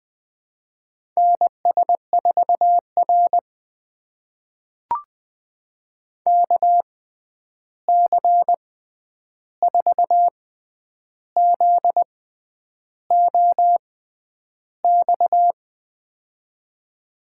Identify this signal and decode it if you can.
Morse code, heard as a keyed tone, about 20 wpm, spaced as for 9 wpm, 700 Hz. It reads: NS4R KC4ZOX
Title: USA Call Signs - 6x Character Spacing - 20wpm